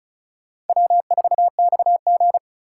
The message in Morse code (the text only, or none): W4XG